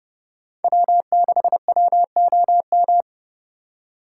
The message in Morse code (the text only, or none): W6WOM